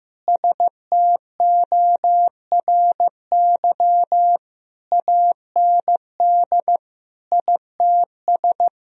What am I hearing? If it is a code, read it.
Morse code, heard as a keyed tone, about 15 wpm, 700 Hz. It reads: STORY AND ITS
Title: Sentences from Top 1000 Words - Review Part 4 - 15wpm